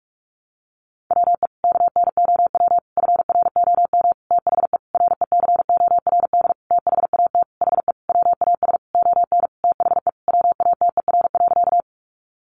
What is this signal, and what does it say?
Morse code, heard as a keyed tone, about 44 wpm, 700 Hz. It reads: WE KNOW FROM THE RECORD THAT HE WAS ON THE WATER.
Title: Sentences from Top 500 Words 45wpm